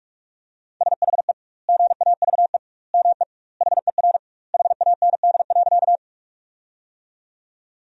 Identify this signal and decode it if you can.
Morse code, heard as a keyed tone, about 42 wpm, 700 Hz. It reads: SHE GAVE ME HER HAND.
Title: 2x Word-Spacing - Sentences from Top 500 Words - 45wpm